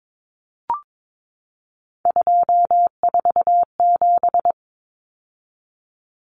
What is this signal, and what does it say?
Morse code, heard as a keyed tone, about 22 wpm, 700 Hz. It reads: 247
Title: Numbers - 3-Digit 22wpm